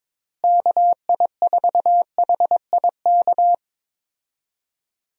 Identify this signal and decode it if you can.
Morse code, heard as a keyed tone, about 22 wpm, 700 Hz. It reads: KI4HIK
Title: USA Call Signs - 6 Characters - Repeated 3x - 22wpm